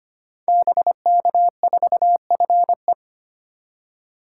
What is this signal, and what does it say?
Morse code, heard as a keyed tone, about 25 wpm, 700 Hz. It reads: BK4FE
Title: Code Groups - 5 Letter:Numbers - 25wpm